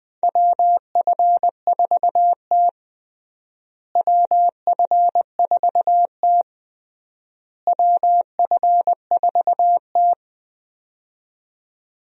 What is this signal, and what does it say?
Morse code, heard as a keyed tone, about 21 wpm, 700 Hz. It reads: WF4T WF4T WF4T
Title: USA Call Signs - 4 Characters - Repeated 3x - 20wpm